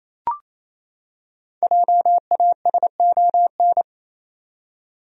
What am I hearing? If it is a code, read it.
Morse code, heard as a keyed tone, about 27 wpm, 700 Hz. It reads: JASON
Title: USA Common Names - Max 5 Letters - 28wpm